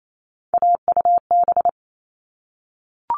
Morse code, heard as a keyed tone, about 28 wpm, 700 Hz. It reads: AUB
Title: All Cut Numbers - 3-Digit 28wpm